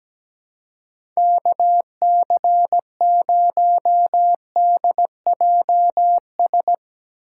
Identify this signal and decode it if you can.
Morse code, heard as a keyed tone, about 17 wpm, 700 Hz. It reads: KC0DJS
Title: USA Call Signs - 6 Characters - 4x Character Spacing - 17wpm